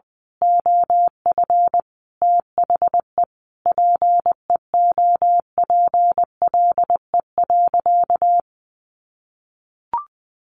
Morse code, heard as a keyed tone, about 20 wpm, 700 Hz. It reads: OF THE PEOPLE.
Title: Sentences from Top 1500 Words 20wpm